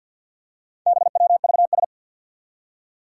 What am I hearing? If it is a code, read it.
Morse code, heard as a keyed tone, about 46 wpm, 700 Hz. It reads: BWVS
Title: Four Challenging Letter-Number Combinations 50wpm